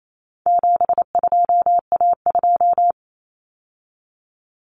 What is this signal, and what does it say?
Morse code, heard as a keyed tone, about 28 wpm, 700 Hz. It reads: 72A2
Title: Common Cut Numbers - 4-Digit 28wpm